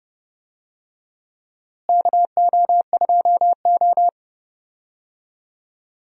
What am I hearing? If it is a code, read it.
Morse code, heard as a keyed tone, about 30 wpm, 700 Hz. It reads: KO2O
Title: USA Call Signs - 4 Characters - Repeated 2x - 30wpm